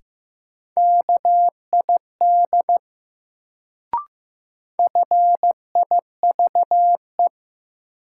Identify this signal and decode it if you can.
Morse code, heard as a keyed tone, about 15 wpm, 700 Hz. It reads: KID FIVE
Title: Sets of 5 Words - New Word Drill - 15wpm